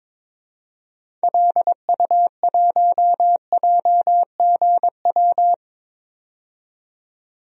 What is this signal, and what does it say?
Morse code, heard as a keyed tone, about 22 wpm, 700 Hz. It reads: LU1JGW
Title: DXCC-100 Call Signs - 6 Characters - 22wpm